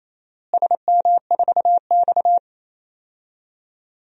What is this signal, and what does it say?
Morse code, heard as a keyed tone, about 28 wpm, 700 Hz. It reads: SM4X
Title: DXCC-100 Call Signs - 4 Characters - Repeated 3x - 28wpm